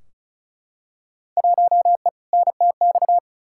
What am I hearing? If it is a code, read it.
Morse code, heard as a keyed tone, about 36 wpm, 700 Hz. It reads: 1E NTX
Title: Field Day Exchanges - 35wpm